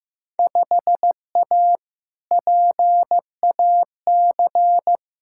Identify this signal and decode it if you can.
Morse code, heard as a keyed tone, about 15 wpm, 700 Hz. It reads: 5A PAC
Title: Field Day Exchanges Encore - 15wpm